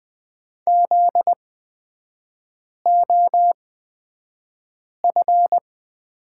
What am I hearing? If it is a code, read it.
Morse code, heard as a keyed tone, about 20 wpm, spaced as for 5 wpm, 700 Hz. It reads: ZOF
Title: 243 Lesson 39e - Letter z - Three Characters 20wpm